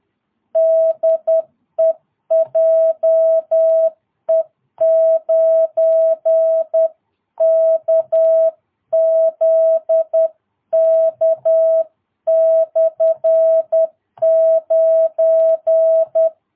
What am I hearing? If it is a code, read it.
Morse code, heard as a keyed tone, about 10 wpm, 660 Hz. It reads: DE JE9KZK/9
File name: jh8cbh / 2022cw